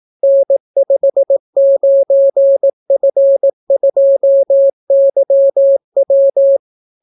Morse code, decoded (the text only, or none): N59F2YW